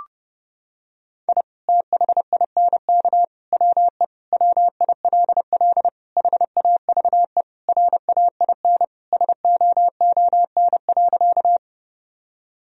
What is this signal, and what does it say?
Morse code, heard as a keyed tone, about 30 wpm, 700 Hz. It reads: I THINK WE WILL HAVE RAIN SOON.
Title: Sentences from Top 1000 Words - Review Part 1 - 30wpm